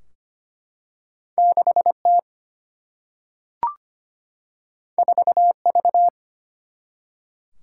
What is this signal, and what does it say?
Morse code, heard as a keyed tone, about 25 wpm, 700 Hz. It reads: 6T 4V
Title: Single-Letter Single-Number 25wpm